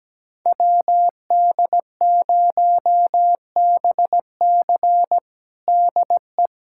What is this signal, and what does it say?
Morse code, heard as a keyed tone, about 17 wpm, 700 Hz. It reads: WD0BC DE
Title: USA Callsign DE Callsign - 5 Characters - 17wpm